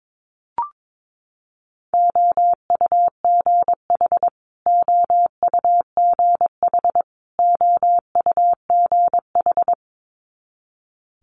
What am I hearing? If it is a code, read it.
Morse code, heard as a keyed tone, about 22 wpm, 700 Hz. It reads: OUGH OUGH OUGH
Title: Multi-letter Phonemes Repeated 3x 22wpm